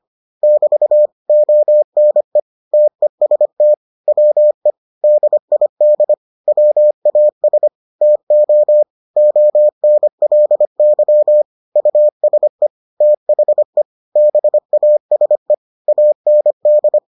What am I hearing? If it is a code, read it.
Morse code, heard as a keyed tone, about 25 wpm, 600 Hz. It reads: <BT> ONE TEST WE DID WAS TO ONLY USE THE BASE AND